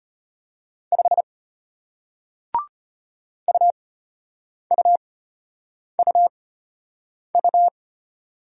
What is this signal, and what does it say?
Morse code, heard as a keyed tone, about 33 wpm, spaced as for 8 wpm, 700 Hz. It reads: 5 UUUU